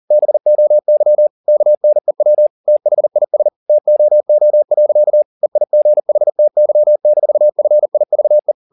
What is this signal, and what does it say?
Morse code, heard as a keyed tone, about 40 wpm, 600 Hz. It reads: BOY KNEW THIS TOO. EIGHTY-FIVE